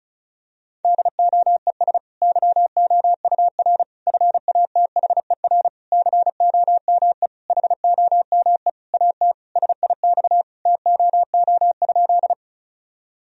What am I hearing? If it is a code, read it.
Morse code, heard as a keyed tone, about 34 wpm, 700 Hz. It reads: DOES YOUR FATHER COME HOME AT SIX TOO?
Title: Sentences from Top 1000 Words - Review Part 2 - 35wpm